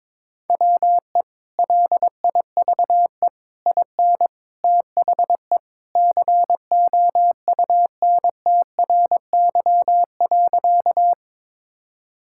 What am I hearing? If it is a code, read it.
Morse code, heard as a keyed tone, about 21 wpm, 700 Hz. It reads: WE LIVE IN THE COUNTRY.
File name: practice / 22wpm / Sentences with 3-Word Phrases - 22wpm